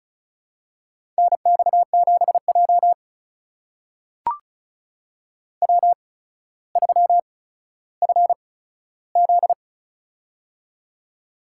Morse code, heard as a keyed tone, about 36 wpm, spaced as for 15 wpm, 700 Hz. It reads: NX7J W3FZ
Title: USA Call Signs - 4 Characters - 8x Character Spacing - 35wpm